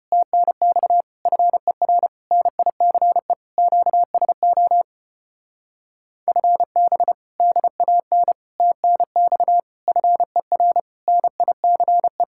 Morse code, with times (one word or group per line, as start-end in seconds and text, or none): 0.1-1.0 TNX
1.3-2.1 FER
2.3-3.3 NICE
3.6-4.8 QSO
6.3-7.1 FB
7.4-8.3 DAN
8.6-9.6 TNX
9.9-10.8 FER
11.1-12.2 NICE